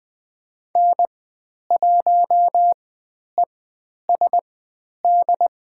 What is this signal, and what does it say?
Morse code, heard as a keyed tone, about 20 wpm, spaced as for 10 wpm, 700 Hz. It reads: N1ESD
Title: 095 Lesson 16g - Letter c - Call Signs 20wpm